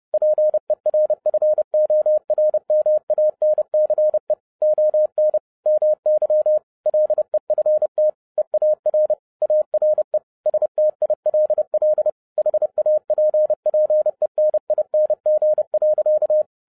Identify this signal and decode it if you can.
Morse code, heard as a keyed tone, about 30 wpm, 610 Hz. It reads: PERFORMANCE ON MY LEFT EAR ARE STILL HAPPENING.
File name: 2011-07-mp3 / n60 BACK FROM A BIG ACCIDENT 30wpm